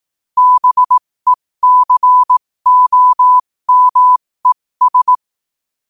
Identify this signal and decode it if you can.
Morse code, heard as a keyed tone, about 18 wpm, spaced as for 15 wpm, 1000 Hz. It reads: BECOMES
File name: qotdfiles / 15wpm